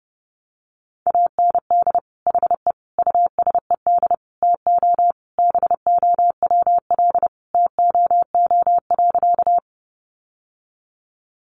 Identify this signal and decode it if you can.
Morse code, heard as a keyed tone, about 30 wpm, 700 Hz. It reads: AND HE USED TO BOWL TOO.
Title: Sentences from Top 1200 Words 30wpm